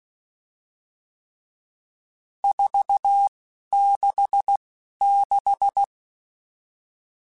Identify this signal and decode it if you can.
Morse code, heard as a keyed tone, about 16 wpm, spaced as for 11 wpm, 800 Hz. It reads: 466
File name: RSTs / RST-mp3s / Various RST @10